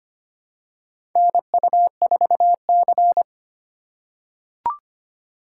Morse code, heard as a keyed tone, about 25 wpm, 700 Hz. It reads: NU4C
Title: USA Call Signs - 4 Characters - 3x Character Spacing - 25wpm